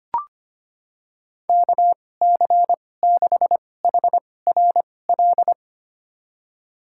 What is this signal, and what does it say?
Morse code, heard as a keyed tone, about 25 wpm, spaced as for 18 wpm, 700 Hz. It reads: KC6HRL
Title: USA Call Signs - 6 Characters - 2x Character Spacing - 25wpm